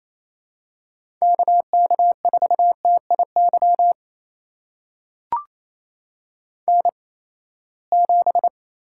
Morse code, heard as a keyed tone, about 29 wpm, spaced as for 20 wpm, 700 Hz. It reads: KK4TIY N7
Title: USA Call Signs - 8x Character Spacing - 28wpm